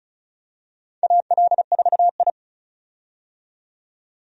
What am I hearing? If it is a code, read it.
Morse code, heard as a keyed tone, about 36 wpm, 700 Hz. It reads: AL4I
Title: USA Call Signs - 4 Characters - 35wpm